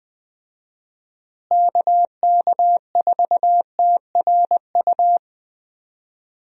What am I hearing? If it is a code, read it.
Morse code, heard as a keyed tone, about 20 wpm, 700 Hz. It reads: KK4TRU
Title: USA Call Signs - 6 Characters - Repeated 3x - 20wpm